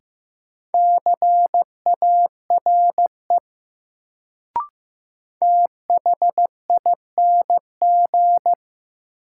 Morse code, heard as a keyed tone, about 15 wpm, 700 Hz. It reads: CARE THING
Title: Sets of 3 Words - New Word Drill - 15wpm